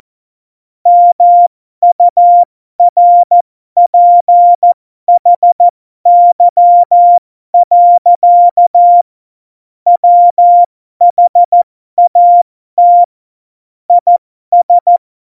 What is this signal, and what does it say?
Morse code, heard as a keyed tone, about 14 wpm, spaced as for 12 wpm, 700 Hz. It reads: MURPHY. WHAT IS
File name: Morsekurs / Witze